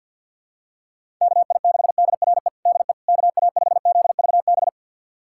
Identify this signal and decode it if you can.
Morse code, heard as a keyed tone, about 51 wpm, 700 Hz. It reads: KI6DRE DE KA5ZVB
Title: USA Callsign DE Callsign 50wpm